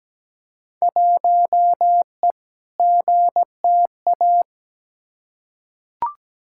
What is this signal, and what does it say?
Morse code, heard as a keyed tone, about 17 wpm, 700 Hz. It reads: 1E GTA
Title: Field Day Exchanges - 17wpm